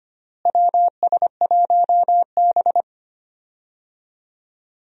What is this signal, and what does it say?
Morse code, heard as a keyed tone, about 25 wpm, 700 Hz. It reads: WS1B
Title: USA Call Signs - 4 Characters - Repeated 2x - 25wpm